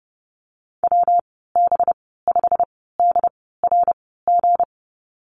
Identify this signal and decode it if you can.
Morse code, heard as a keyed tone, about 30 wpm, spaced as for 17 wpm, 700 Hz. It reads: WB5DRG
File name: Lesson 37g - Letter x - Call Signs 30wpm